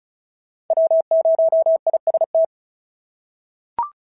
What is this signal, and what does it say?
Morse code, heard as a keyed tone, about 36 wpm, 690 Hz. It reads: W0IST
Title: SOTA QSO Practice 35wpm